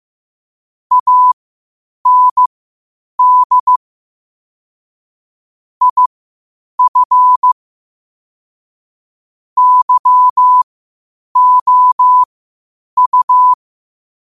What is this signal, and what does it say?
Morse code, heard as a keyed tone, about 15 wpm, spaced as for 8 wpm, 1000 Hz. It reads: AND IF YOU
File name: qotdfiles / 8wpm